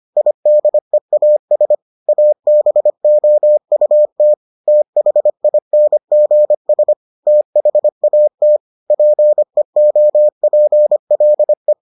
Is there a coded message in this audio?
Yes: Morse code from 0.2 to 11.7 s, heard as a keyed tone, about 25 wpm, 600 Hz. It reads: IDEAS ABOUT THINGS THAT PEOPLE